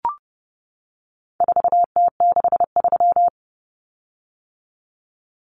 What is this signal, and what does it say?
Morse code, heard as a keyed tone, about 30 wpm, 700 Hz. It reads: 4T63
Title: Common Cut Numbers - 4-Digit 30wpm